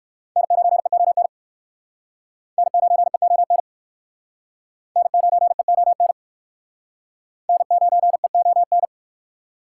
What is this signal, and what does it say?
Morse code, heard as a keyed tone, about 58 wpm, 700 Hz. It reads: N9EON N9EON N9EON N9EON